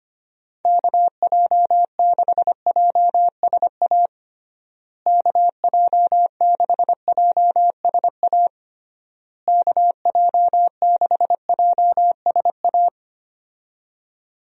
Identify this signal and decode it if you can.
Morse code, heard as a keyed tone, about 25 wpm, 700 Hz. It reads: KJ6JSA KJ6JSA KJ6JSA
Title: USA Call Signs - 6 Characters - Repeated 3x - 25wpm